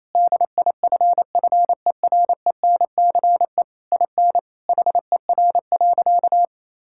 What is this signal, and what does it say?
Morse code, heard as a keyed tone, about 28 wpm, 700 Hz. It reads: DIFFERENCE IN HER.
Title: Sentences from Top 1000 Words - Review Part 5 - 28wpm